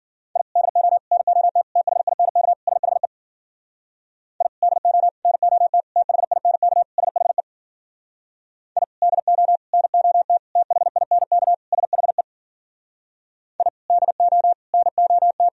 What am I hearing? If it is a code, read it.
Morse code, heard as a keyed tone, about 51 wpm, 700 Hz. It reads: I DO NOT THINK SHE I DO NOT THINK SHE I DO NOT THINK SHE I DO NOT